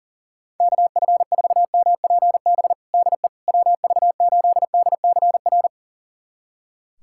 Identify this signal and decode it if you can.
Morse code, heard as a keyed tone, about 40 wpm, 700 Hz. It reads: KF4MPB DE WV8DCR
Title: USA Callsign DE Callsign 40wpm